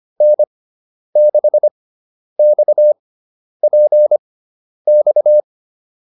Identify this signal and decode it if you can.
Morse code, heard as a keyed tone, about 25 wpm, spaced as for 10 wpm, 600 Hz. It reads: N6XPX
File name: QSO306 10